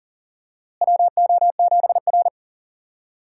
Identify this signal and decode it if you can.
Morse code, heard as a keyed tone, about 40 wpm, 700 Hz. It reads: WO7R
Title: USA Call Signs - 4 Characters - Repeated 3x - 40wpm